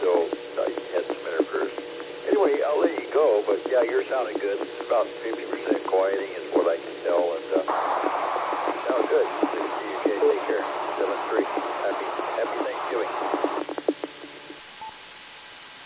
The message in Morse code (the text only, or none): none